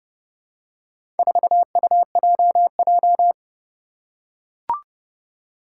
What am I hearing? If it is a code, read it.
Morse code, heard as a keyed tone, about 31 wpm, 700 Hz. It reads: 4UJJ
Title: Four Challenging Letter-Number Combinations 30wpm